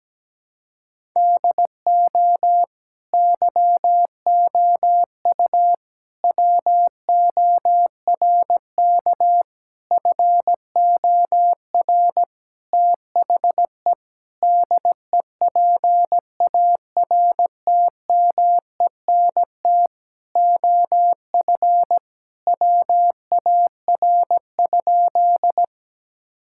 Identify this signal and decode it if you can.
Morse code, heard as a keyed tone, about 17 wpm, 700 Hz. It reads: DO YOU WORK FOR THE DEPARTMENT OF WAR?
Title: Sentences from Top 1000 Words - Review Part 5 - 17wpm